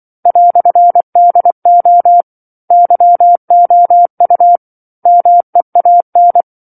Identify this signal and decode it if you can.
Morse code, heard as a keyed tone, about 24 wpm, 700 Hz. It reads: "DO YOU MEAN
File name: 07-Chapter-07 24 WPM